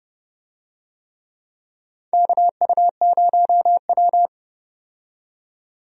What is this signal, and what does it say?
Morse code, heard as a keyed tone, about 30 wpm, 700 Hz. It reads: KU0W
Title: USA Call Signs - 4 Characters - 30wpm